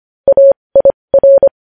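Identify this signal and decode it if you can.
Morse code, heard as a keyed tone, about 25 wpm, spaced as for 20 wpm, 550 Hz. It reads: AIR
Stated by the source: News headlines sent in Morse code at 20.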